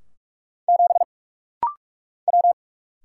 Morse code, heard as a keyed tone, about 46 wpm, 700 Hz. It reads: 7 W
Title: Single Letter-Number Alt - Rapid-Fire 45wpm